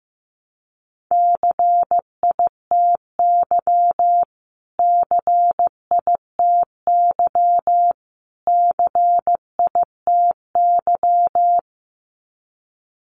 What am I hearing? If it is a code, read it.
Morse code, heard as a keyed tone, about 15 wpm, 700 Hz. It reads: CITY CITY CITY
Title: Top 500 Words Repeated 15wpm